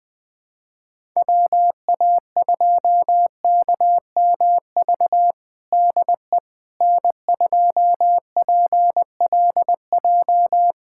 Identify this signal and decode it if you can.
Morse code, heard as a keyed tone, about 21 wpm, 700 Hz. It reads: WA2KMV DE N2PLJ
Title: USA Callsign DE Callsign 20wpm